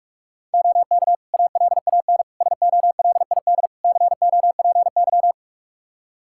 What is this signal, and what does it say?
Morse code, heard as a keyed tone, about 46 wpm, 700 Hz. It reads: OK ALAN SOLID COPY
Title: Practice QSO - Exchange 3 - More Info - 45wpm